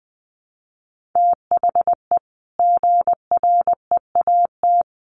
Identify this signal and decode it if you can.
Morse code, heard as a keyed tone, about 20 wpm, 700 Hz. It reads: THE GREAT
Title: Sentences from Top 1500 Words 20wpm